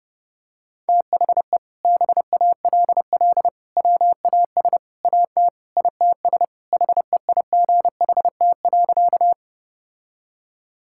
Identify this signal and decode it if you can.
Morse code, heard as a keyed tone, about 32 wpm, 700 Hz. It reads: THE BALL WAS AT ITS HEIGHT.
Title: Sentences from Top 1600 Words 30wpm